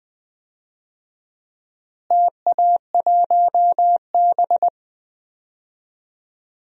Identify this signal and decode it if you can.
Morse code, heard as a keyed tone, about 20 wpm, 700 Hz. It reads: TA1B
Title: DXCC-100 Call Signs - 4 Characters - Repeated 3x - 20wpm